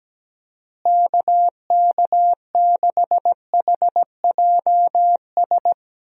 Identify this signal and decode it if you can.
Morse code, heard as a keyed tone, about 17 wpm, 700 Hz. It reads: KK6HJS
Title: USA Call Signs - 6 Characters - Repeated 2x - 17wpm